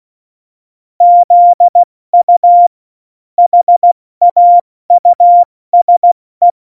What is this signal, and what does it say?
Morse code, heard as a keyed tone, about 16 wpm, 700 Hz. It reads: ZU HAUSE
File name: Morsekurs / Witze / Witz 27 Juni 17, Tempo 70 F80